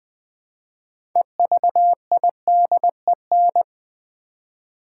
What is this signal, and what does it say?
Morse code, heard as a keyed tone, about 20 wpm, 700 Hz. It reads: EVIDEN
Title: Build A Word - 2 Letters At A Time - 8-Letter Words - 20wpm